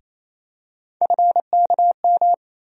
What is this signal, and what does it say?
Morse code, heard as a keyed tone, about 28 wpm, 700 Hz. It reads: FKM
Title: USA Call Sign Postfixes - 3 Letters 28wpm